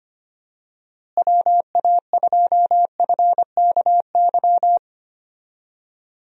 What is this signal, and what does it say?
Morse code, heard as a keyed tone, about 25 wpm, 700 Hz. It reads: WA2FKY
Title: USA Call Signs - 6 Characters - Repeated 3x - 25wpm